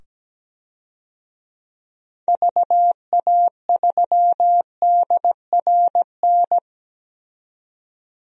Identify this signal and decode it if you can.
Morse code, heard as a keyed tone, about 17 wpm, 700 Hz. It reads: VA3DRN
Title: DXCC-100 Call Signs - 6 Characters - 17wpm